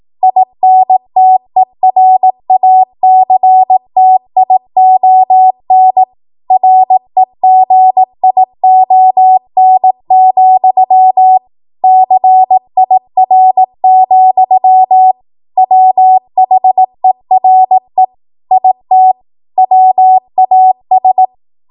Morse code, heard as a keyed tone, about 18 wpm, 750 Hz. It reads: INTERACTION REGION, CIR, WHERE IT WAS
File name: Morse / 18WPM